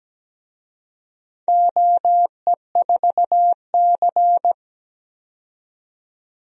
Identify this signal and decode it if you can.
Morse code, heard as a keyed tone, about 17 wpm, 700 Hz. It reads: OE4C
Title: DXCC-100 Call Signs - 4 Characters - 17wpm